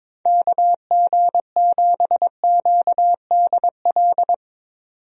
Morse code, heard as a keyed tone, about 22 wpm, 700 Hz. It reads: KG7QDL
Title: USA Call Signs - 6 Characters - 22wpm